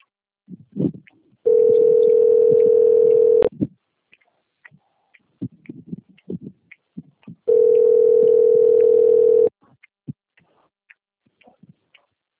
Ringback tone (North America), 440 + 480 Hz.